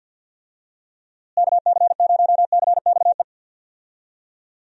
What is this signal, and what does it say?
Morse code, heard as a keyed tone, about 51 wpm, 700 Hz. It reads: KC0CXE
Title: USA Call Signs - 6 Characters - Repeated 2x - 50wpm